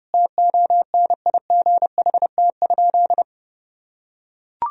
Morse code, heard as a keyed tone, about 32 wpm, 700 Hz. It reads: TONIGHT?
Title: Sentences from Top 1000 Words - Review Part 2 - 30wpm